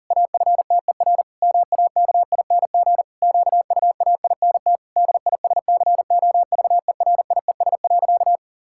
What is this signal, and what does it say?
Morse code, heard as a keyed tone, about 40 wpm, 700 Hz. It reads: AFTER MAKING QUAINT DISCOVERIES.
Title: Pangram Challenge v2 - 40wpm